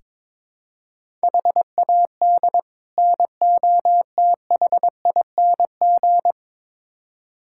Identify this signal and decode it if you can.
Morse code, heard as a keyed tone, about 22 wpm, 700 Hz. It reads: HAD NOTHING
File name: practice / 22wpm / Build A Phrase - 5-Word Phrases - 22wpm